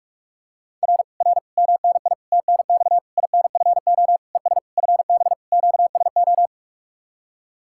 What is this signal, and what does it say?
Morse code, heard as a keyed tone, about 45 wpm, 700 Hz. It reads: R R MNI TNX INFO ES FB QSO